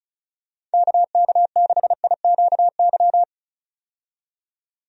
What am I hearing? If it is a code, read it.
Morse code, heard as a keyed tone, about 35 wpm, 700 Hz. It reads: KK6IQY
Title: USA Call Signs - 6 Characters - 35wpm